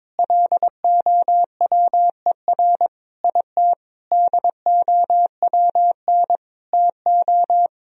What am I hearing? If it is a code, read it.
Morse code, heard as a keyed tone, about 22 wpm, 700 Hz. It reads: LOWER IT DOWN TO
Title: Sentences from Top 1000 Words - Review Part 4 - 22wpm